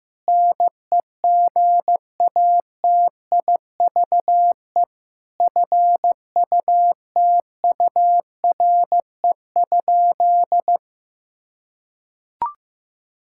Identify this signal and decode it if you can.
Morse code, heard as a keyed tone, about 16 wpm, 700 Hz. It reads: NEGATIVE FUTURE?
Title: Sentences from Top 1000 Words - Review Part 6 - 15wpm